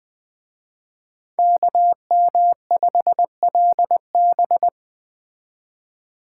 Morse code, heard as a keyed tone, about 20 wpm, 700 Hz. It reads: KM5LB
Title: Speed Racing - USA 2x2 Call Signs 17wpm